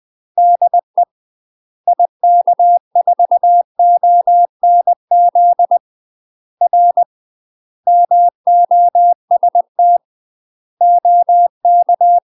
Morse code, heard as a keyed tone, about 20 wpm, 700 Hz. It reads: DE IK4ONZ R MOST OK
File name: UK MASTER QSOs TRAINING / 20WPM / UK MASTER QSOs 0066